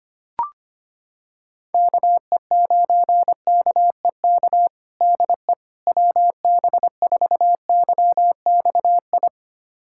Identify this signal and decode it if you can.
Morse code, heard as a keyed tone, about 25 wpm, 700 Hz. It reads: KE9KEK DE WB4YXI
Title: USA Callsign DE Callsign - 6 Characters - 25wpm